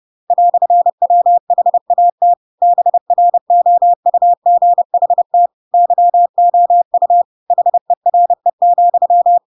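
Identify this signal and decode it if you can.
Morse code, heard as a keyed tone, about 30 wpm, 700 Hz. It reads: "WHAT BROUGHT YOU HERE,
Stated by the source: Chapter 2 of The Adventures of Pinocchio read as a book in Morse code (CW) at 30 wpm, by ik1hge.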